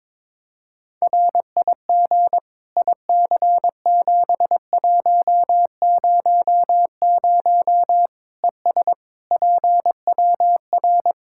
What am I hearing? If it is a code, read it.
Morse code, heard as a keyed tone, about 21 wpm, 700 Hz. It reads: RIG IC7100 ES PWR
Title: K1USN SST QSO Practice 22wpm